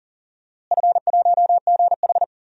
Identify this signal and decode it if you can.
Morse code, heard as a keyed tone, about 37 wpm, 700 Hz. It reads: F1GH
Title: DXCC-100 Call Signs - 4 Characters - Repeated 2x - 40wpm